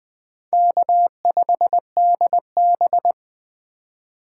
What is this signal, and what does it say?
Morse code, heard as a keyed tone, about 20 wpm, 700 Hz. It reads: K5DB
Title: USA Call Signs - 4 Characters - 4x Character Spacing - 20wpm